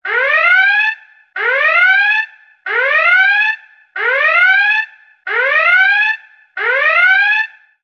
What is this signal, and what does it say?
Emergency Alert System (SAME) tones (header bursts), heard as harsh buzzing bursts. It is heard from 0.4 to 7.3 s.
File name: emergency-alert-system-ringtone